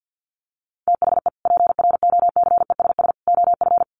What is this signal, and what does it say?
Morse code, heard as a keyed tone, about 48 wpm, 700 Hz. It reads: THE PROCESS OF
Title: Sets of 3 Words 50wpm